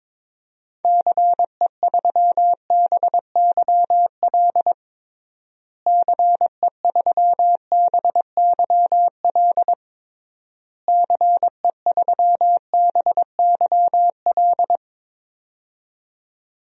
Morse code, heard as a keyed tone, about 22 wpm, 700 Hz. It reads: CE3BYL CE3BYL CE3BYL
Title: DXCC-100 Call Signs - 6 Characters - Repeated 3x - 22wpm